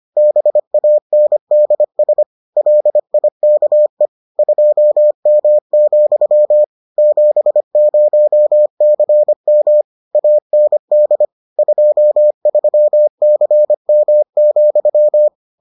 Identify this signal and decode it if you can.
Morse code, heard as a keyed tone, about 25 wpm, 600 Hz. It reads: BANDS LIKE 2M, 70CM AND 23CM,